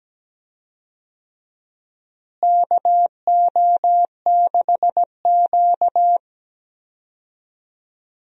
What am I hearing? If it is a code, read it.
Morse code, heard as a keyed tone, about 17 wpm, 700 Hz. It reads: KO6Q
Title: USA Call Signs - 4 Characters - Repeated 2x - 17wpm